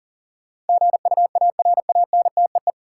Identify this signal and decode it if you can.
Morse code, heard as a keyed tone, about 40 wpm, 700 Hz. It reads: GUARANTEE
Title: Sentences from Top 1600 Words - New Word Drill - 40wpm